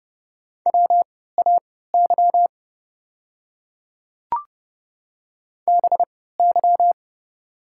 Morse code, heard as a keyed tone, about 30 wpm, spaced as for 17 wpm, 690 Hz. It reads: WAY BY
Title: 3x Character Spacing - Top 100 Words - 30wpm